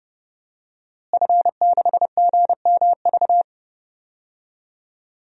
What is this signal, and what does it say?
Morse code, heard as a keyed tone, about 30 wpm, 700 Hz. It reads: F6GMV
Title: DXCC-100 Call Signs - 5 Characters - Repeated 3x - 30wpm